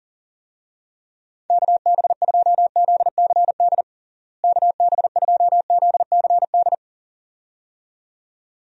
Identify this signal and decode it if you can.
Morse code, heard as a keyed tone, about 40 wpm, 700 Hz. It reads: KB2ZCD KB2ZCD
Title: USA Call Signs - 6 Characters - Repeated 2x - 40wpm